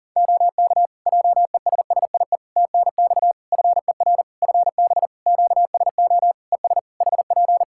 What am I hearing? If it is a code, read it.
Morse code, heard as a keyed tone, about 40 wpm, 700 Hz. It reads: OK JESSIE TNX FER FB QSO ES HP
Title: Practice QSO - Exchange 4 - Ending - 40wpm